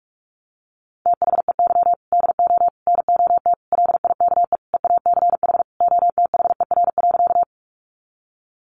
Morse code, heard as a keyed tone, about 44 wpm, 700 Hz. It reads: THEY DO NOT LIKE EACH OTHER.
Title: Easy Phrases 45wpm